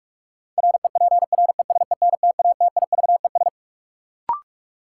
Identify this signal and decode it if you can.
Morse code, heard as a keyed tone, about 46 wpm, 700 Hz. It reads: REPRESENTATIVES
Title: Sets of 5 Words Encore - New Word Drill - 45wpm